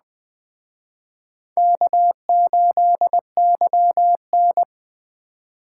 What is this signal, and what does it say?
Morse code, heard as a keyed tone, about 20 wpm, 700 Hz. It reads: K8YN